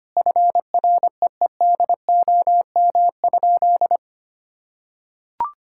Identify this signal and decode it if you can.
Morse code, heard as a keyed tone, about 25 wpm, 700 Hz. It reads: FREEDOM?
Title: Sentences from Top 1600 Words 25wpm